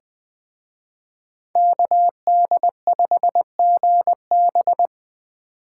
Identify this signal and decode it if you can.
Morse code, heard as a keyed tone, about 21 wpm, 700 Hz. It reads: KD5GB